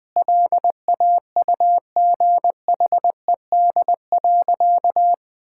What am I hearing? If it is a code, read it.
Morse code, heard as a keyed tone, about 20 wpm, 700 Hz. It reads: LAUGHED.
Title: Sentences with 2-Word Phrases - 20wpm